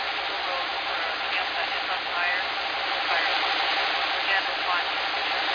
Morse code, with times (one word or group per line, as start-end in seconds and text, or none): none